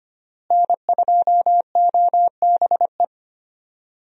Morse code, heard as a keyed tone, about 25 wpm, 700 Hz. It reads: N2OBE
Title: USA Call Signs - 5 Characters - Repeated 3x - 25wpm